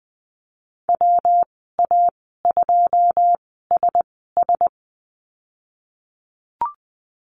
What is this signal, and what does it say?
Morse code, heard as a keyed tone, about 20 wpm, spaced as for 15 wpm, 700 Hz. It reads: WA2SS
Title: 140 Lesson 23g - Letter f - Call Signs 20wpm